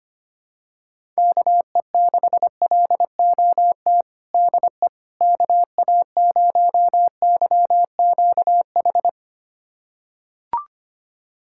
Morse code, heard as a keyed tone, about 25 wpm, 700 Hz. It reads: KE6LOT DE KA0YQH
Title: USA Callsign DE Callsign 25wpm